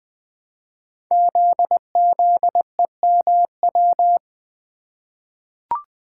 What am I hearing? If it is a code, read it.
Morse code, heard as a keyed tone, about 20 wpm, 700 Hz. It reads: ZZEMW
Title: Code Groups - 5 Letters - 20wpm